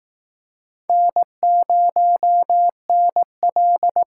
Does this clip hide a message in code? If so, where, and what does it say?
Yes: Morse code from 0.9 to 4.0 s, heard as a keyed tone, about 19 wpm, 700 Hz. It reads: N0NL